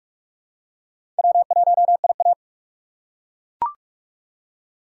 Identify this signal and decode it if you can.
Morse code, heard as a keyed tone, about 42 wpm, 700 Hz. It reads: W1IA
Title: USA Call Signs - 4 Characters - 2x Character Spacing - 45wpm